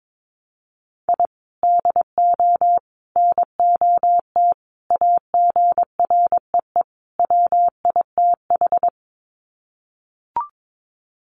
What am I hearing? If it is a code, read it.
Morse code, heard as a keyed tone, about 22 wpm, 700 Hz. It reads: I DO NOT AGREE WITH
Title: Sets of 5 Words Encore 22wpm